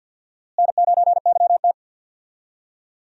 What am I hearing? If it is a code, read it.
Morse code, heard as a keyed tone, about 53 wpm, 690 Hz. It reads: N9YT
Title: USA Call Signs - 4 Characters - Repeated 2x - 50wpm